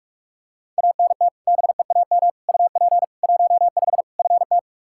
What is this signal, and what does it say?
Morse code, heard as a keyed tone, about 46 wpm, 700 Hz. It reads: ANT BEAM UP 15 FT
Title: Practice QSO - Exchange 2 - Info - 45wpm